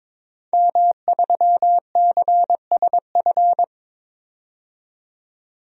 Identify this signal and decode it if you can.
Morse code, heard as a keyed tone, about 22 wpm, 700 Hz. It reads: M3CSF
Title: DXCC-100 Call Signs - 5 Characters - 22wpm